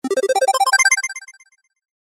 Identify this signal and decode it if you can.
There is no code.